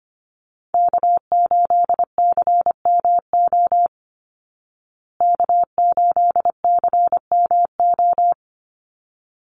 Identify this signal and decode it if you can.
Morse code, heard as a keyed tone, about 24 wpm, 700 Hz. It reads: K8CMO K8CMO
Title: Call Signs Repeated 3x - 25wpm